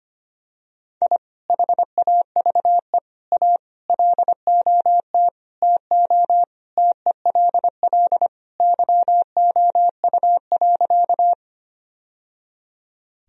Morse code, heard as a keyed tone, about 25 wpm, 700 Hz. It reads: I HAVE A LOT TO TELL YOU.
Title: Sentences from Top 1000 Words - Review Part 3 - 25wpm